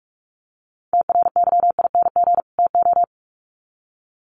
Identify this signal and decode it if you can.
Morse code, heard as a keyed tone, about 44 wpm, 700 Hz. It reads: TRYING TO
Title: Sets of 2 Words 45wpm